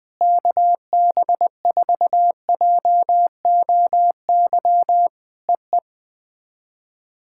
Morse code, heard as a keyed tone, about 20 wpm, 700 Hz. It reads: KB4JOY EE
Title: POTA QSO Practice - 20wpm